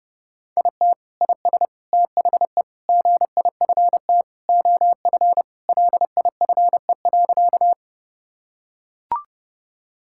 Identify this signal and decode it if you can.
Morse code, heard as a keyed tone, about 30 wpm, 700 Hz. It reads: IT IS THE GIFT OF LIFE.
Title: Sentences from Top 1000 Words - Review Part 6 - 30wpm